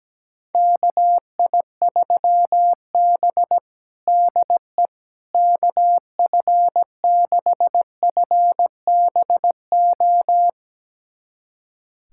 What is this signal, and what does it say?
Morse code, heard as a keyed tone, about 17 wpm, 700 Hz. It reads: KI3B DE KF6FBO
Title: USA Callsign DE Callsign 17wpm